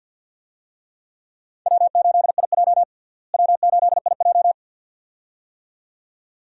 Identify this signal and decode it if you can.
Morse code, heard as a keyed tone, about 47 wpm, 700 Hz. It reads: W8IJ W8IJ
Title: USA Call Signs - 4 Characters - Repeated 2x - 50wpm